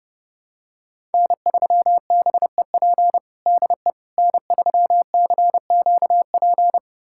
Morse code, heard as a keyed tone, about 32 wpm, 700 Hz. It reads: N3BEP DE N3CQP
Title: USA Callsign DE Callsign - 5 Characters - 30wpm